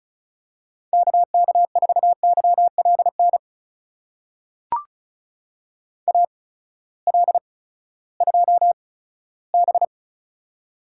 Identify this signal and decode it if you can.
Morse code, heard as a keyed tone, about 36 wpm, spaced as for 18 wpm, 700 Hz. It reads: KK4YLN AL2B
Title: USA Call Signs - 8x Character Spacing - 35wpm